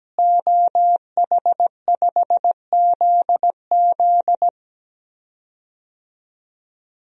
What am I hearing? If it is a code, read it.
Morse code, heard as a keyed tone, about 17 wpm, 700 Hz. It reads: OH5ZZ
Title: DXCC-100 Call Signs - 5 Characters - Repeated 3x - 17wpm